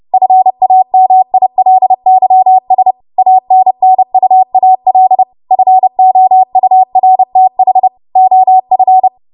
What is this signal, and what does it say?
Morse code, heard as a keyed tone, about 30 wpm, 750 Hz. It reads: FAMILYS ANNUAL FOURTH OF
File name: Morse / 30WPM